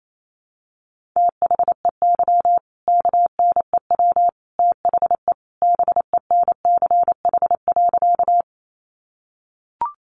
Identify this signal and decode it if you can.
Morse code, heard as a keyed tone, about 28 wpm, 700 Hz. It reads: THEY KNEW THE BENCH.
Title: Sentences from Top 1400 Words 28wpm